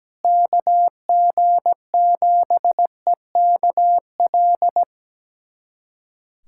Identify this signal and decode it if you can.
Morse code, heard as a keyed tone, about 17 wpm, 700 Hz. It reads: KG7EKL
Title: USA Callsign DE Callsign 17wpm